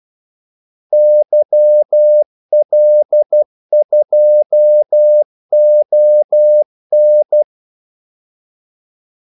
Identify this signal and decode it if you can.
Morse code, heard as a keyed tone, about 12 wpm, 600 Hz. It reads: YL2ON
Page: 2 ekstra ordmellemrum mellem kaldesignalerne